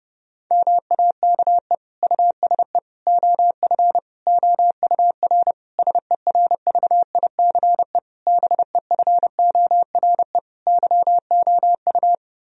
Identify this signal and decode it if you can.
Morse code, heard as a keyed tone, about 32 wpm, 700 Hz. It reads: MAKE USE OF OUR SERVICE BEFORE YOU
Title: Sentences from Top 1000 Words - Review Part 4 - 30wpm